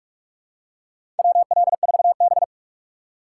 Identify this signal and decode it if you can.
Morse code, heard as a keyed tone, about 46 wpm, 700 Hz. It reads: WL4B
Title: USA Call Signs - 3x Character Spacing - 45wpm